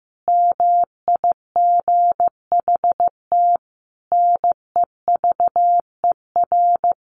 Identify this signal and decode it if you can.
Morse code, heard as a keyed tone, about 15 wpm, 700 Hz. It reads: MIGHT NEVER
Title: A Sound Of Thunder 15wpm